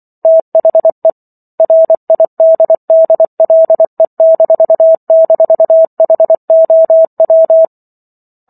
Morse code, heard as a keyed tone, about 24 wpm, 650 Hz. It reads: THE RIDDLE--HOW